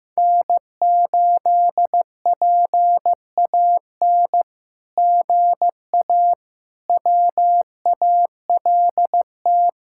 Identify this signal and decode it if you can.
Morse code, heard as a keyed tone, about 15 wpm, 700 Hz. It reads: N8PAN GA WALT
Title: SOTA QSO Practice 15wpm